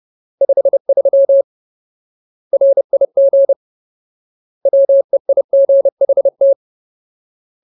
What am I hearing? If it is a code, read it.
Morse code, heard as a keyed tone, about 30 wpm, 550 Hz. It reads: 53 RIG WEIGHT
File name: Ultimate Word List / 30WPM Ultimate